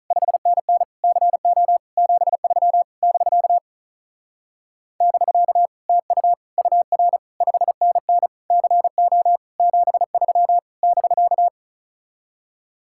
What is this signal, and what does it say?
Morse code, heard as a keyed tone, about 43 wpm, 700 Hz. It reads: 5NN CO 73 <BK> <BK> TU UR 5NN CO 73 <BK>
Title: Speed-Racing Common QSO Phrases Encore - 35wpm